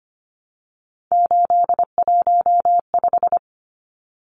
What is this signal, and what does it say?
Morse code, heard as a keyed tone, about 25 wpm, 700 Hz. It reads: 815
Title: Numbers - 3-Digit 25wpm